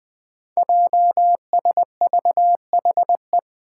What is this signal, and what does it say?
Morse code, heard as a keyed tone, about 20 wpm, 700 Hz. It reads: JSVHE
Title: Five Challenging Letter-Number Combinations 20wpm